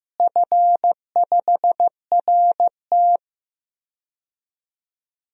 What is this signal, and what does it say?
Morse code, heard as a keyed tone, about 15 wpm, 700 Hz. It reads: F5RT
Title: DXCC-100 Call Signs - 4 Characters - Repeated 2x - 15wpm